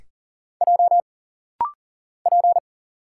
Morse code, heard as a keyed tone, about 40 wpm, 700 Hz. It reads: J P
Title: Single Letter-Number - Rapid-Fire 40wpm